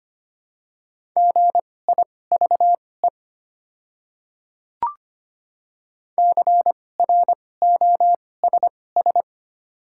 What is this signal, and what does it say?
Morse code, heard as a keyed tone, about 25 wpm, spaced as for 18 wpm, 690 Hz. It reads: GIVE CROSS
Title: 2x Character Spacing - Top 500 Words - 25wpm